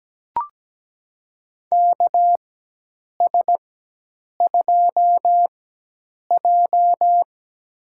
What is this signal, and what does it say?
Morse code, heard as a keyed tone, about 17 wpm, spaced as for 8 wpm, 700 Hz. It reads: KS2J
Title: USA Call Signs - 4 Characters - 4x Character Spacing - 17wpm